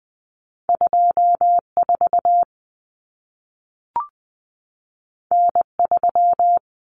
Morse code, heard as a keyed tone, about 20 wpm, 700 Hz. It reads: 24 N3
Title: Common Cut Numbers - 2-Digit 20wpm